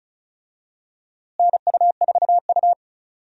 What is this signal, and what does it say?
Morse code, heard as a keyed tone, about 36 wpm, 700 Hz. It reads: NU4U
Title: USA Call Signs - 4 Characters - Repeated 3x - 35wpm